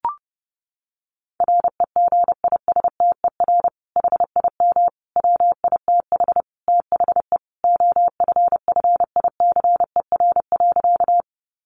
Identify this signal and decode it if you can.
Morse code, heard as a keyed tone, about 30 wpm, 700 Hz. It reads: REGISTER HIM WITH THE OFFICER.
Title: Sentences from Top 1200 Words 30wpm